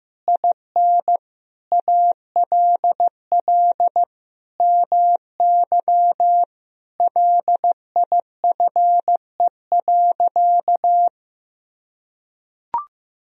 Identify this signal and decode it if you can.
Morse code, heard as a keyed tone, about 15 wpm, 700 Hz. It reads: IN ALL MY LIFE.
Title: Sentences from Top 1000 Words - Review Part 1 - 15wpm